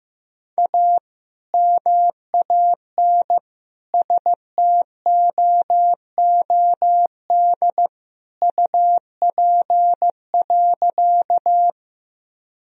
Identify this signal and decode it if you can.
Morse code, heard as a keyed tone, about 15 wpm, 700 Hz. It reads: A MAN STOOD UP.
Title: Sentences from Top 1000 Words - Review Part 2 - 15wpm